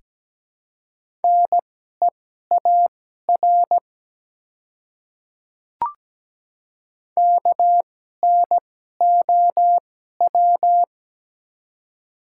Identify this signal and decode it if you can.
Morse code, heard as a keyed tone, about 17 wpm, spaced as for 12 wpm, 700 Hz. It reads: NEAR KNOW
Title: 2x Character Spacing - Top 200 Words - 17wpm